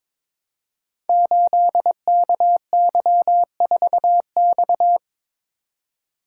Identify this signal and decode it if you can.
Morse code, heard as a keyed tone, about 22 wpm, 700 Hz. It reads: 8KY4X
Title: Code Groups - 5 Letter:Numbers - 22wpm